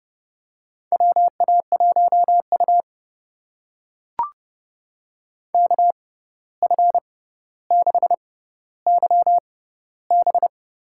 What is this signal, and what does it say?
Morse code, heard as a keyed tone, about 30 wpm, spaced as for 14 wpm, 700 Hz. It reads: WA1U KF6YB
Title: USA Call Signs - 6x Character Spacing - 30wpm